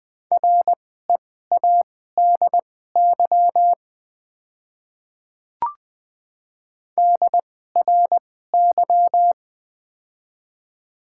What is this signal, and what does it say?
Morse code, heard as a keyed tone, about 20 wpm, spaced as for 15 wpm, 700 Hz. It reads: READY DRY
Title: 2x Character Spacing - Top 500 Words - 20wpm